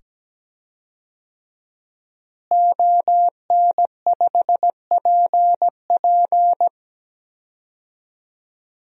Morse code, heard as a keyed tone, about 17 wpm, 700 Hz. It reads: ON5PP
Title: DXCC-100 Call Signs - 5 Characters - Repeated 2x - 17wpm